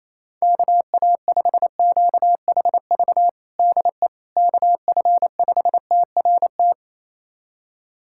Morse code, heard as a keyed tone, about 28 wpm, 700 Hz. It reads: KA5QHV DE KF5TRT
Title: USA Callsign DE Callsign - 6 Characters - 28wpm